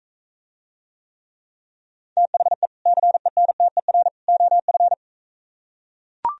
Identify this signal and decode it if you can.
Morse code, heard as a keyed tone, about 43 wpm, 700 Hz. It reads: THE CENTER OF